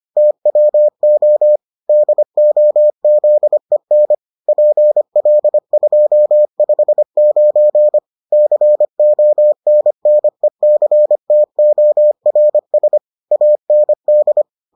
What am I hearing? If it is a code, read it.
Morse code, heard as a keyed tone, about 25 wpm, 600 Hz. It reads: TWO DOZEN PL259 CONNECTORS AND